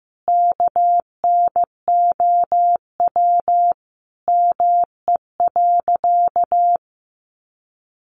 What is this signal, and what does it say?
Morse code, heard as a keyed tone, about 15 wpm, 700 Hz. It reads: KNOW ME.
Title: Sentences from Top 200 Words 15wpm